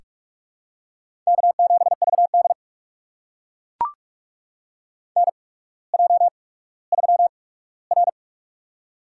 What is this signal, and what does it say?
Morse code, heard as a keyed tone, about 44 wpm, spaced as for 19 wpm, 700 Hz. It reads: K7VD NJ3R
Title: USA Call Signs - 4 Characters - 8x Character Spacing - 45wpm